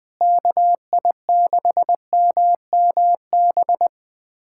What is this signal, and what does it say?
Morse code, heard as a keyed tone, about 19 wpm, 700 Hz. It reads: KI6MMB
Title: USA Callsign DE Callsign 20wpm